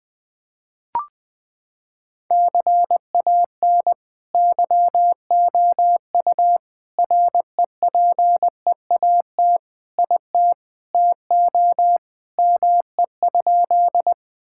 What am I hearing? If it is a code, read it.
Morse code, heard as a keyed tone, about 20 wpm, 700 Hz. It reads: CAN YOU REPEAT IT TO ME?
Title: Sentences from Top 1600 Words 20wpm